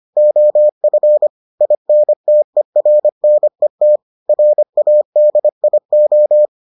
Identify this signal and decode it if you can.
Morse code, heard as a keyed tone, about 25 wpm, 600 Hz. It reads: OF INTERNET RADIO